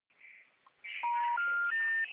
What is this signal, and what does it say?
Special information tone, with noise.
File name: out- -2012- - - .7759